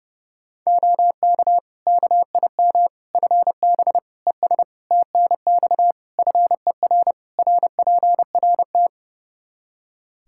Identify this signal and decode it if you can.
Morse code, heard as a keyed tone, about 30 wpm, 700 Hz. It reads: OK KIM FB ES TNX FER RPRT
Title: Practice QSO - Exchange 2 - Info - 30wpm